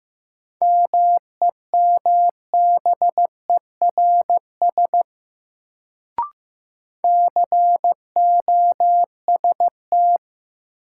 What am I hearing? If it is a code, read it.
Morse code, heard as a keyed tone, about 15 wpm, 700 Hz. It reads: MEMBERS COST
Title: Sets of 3 Words - New Word Drill - 15wpm